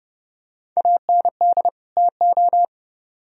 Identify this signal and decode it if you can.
Morse code, heard as a keyed tone, about 30 wpm, 700 Hz. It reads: AND TO